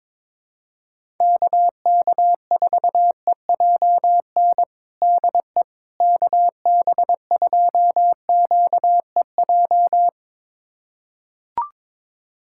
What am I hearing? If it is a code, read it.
Morse code, heard as a keyed tone, about 21 wpm, 700 Hz. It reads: KK4EJN DE KB2QEJ
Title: USA Callsign DE Callsign 22wpm